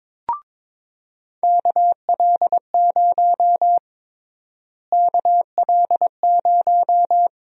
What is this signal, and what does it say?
Morse code, heard as a keyed tone, about 22 wpm, 700 Hz. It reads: KL0 KL0
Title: USA Call Sign Prefixes - Repeated 2x - 22wpm